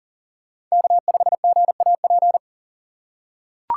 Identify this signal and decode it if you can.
Morse code, heard as a keyed tone, about 40 wpm, 700 Hz. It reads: K5GAP
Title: USA Call Signs - 2x Character Spacing - 40wpm